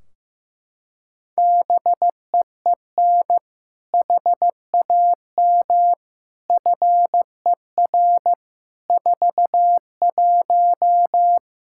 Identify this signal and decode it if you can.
Morse code, heard as a keyed tone, about 15 wpm, 700 Hz. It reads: BEEN HAM FER 41
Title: Practice QSO - Exchange 3 - More Info - 15wpm